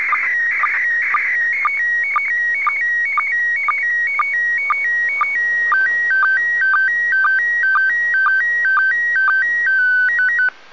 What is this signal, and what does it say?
SSTV (slow-scan television), PD 120.
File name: SSTV-Signal-31.02.19-17.35Uhr-JN39vk-via-DL0HSK-A